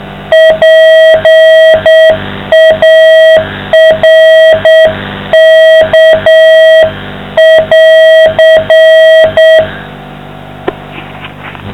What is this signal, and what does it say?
Morse code, heard as a keyed tone, about 7 wpm, 640 Hz, with noise. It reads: PARK<AR>